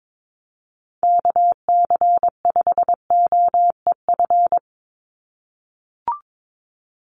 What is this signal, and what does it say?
Morse code, heard as a keyed tone, about 22 wpm, 700 Hz. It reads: KC5OEF
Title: USA Call Signs 22wpm